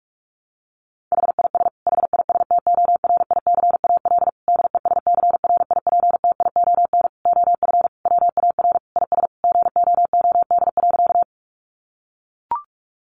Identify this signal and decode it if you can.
Morse code, heard as a keyed tone, about 44 wpm, 700 Hz. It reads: HIS HISTORICAL DESCRIPTION OF WAR IS GOOD.
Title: Sentences from Top 1300 Words 45wpm